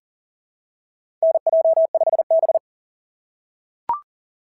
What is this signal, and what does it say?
Morse code, heard as a keyed tone, about 40 wpm, 650 Hz. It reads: NJ5B
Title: SOTA QSO Practice 40wpm